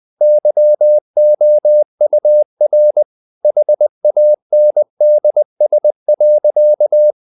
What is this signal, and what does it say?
Morse code, heard as a keyed tone, about 20 wpm, 600 Hz. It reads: YOUR HANDS.
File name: the old man and the sea part 2 20wpm